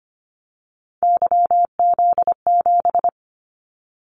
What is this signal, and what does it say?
Morse code, heard as a keyed tone, about 25 wpm, 700 Hz. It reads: YZ7
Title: Callsign Prefixes for Top 25 DXCC - 25wpm